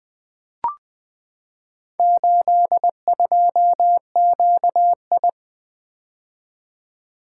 Morse code, heard as a keyed tone, about 20 wpm, 700 Hz. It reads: 82QI
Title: Four Letter-Number Combinations - 20wpm